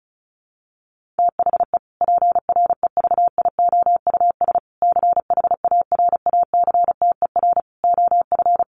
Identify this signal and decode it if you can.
Morse code, heard as a keyed tone, about 34 wpm, 700 Hz. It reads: THE PREVIOUS CHARACTER OF
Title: Sentences from Top 1500 Words 35wpm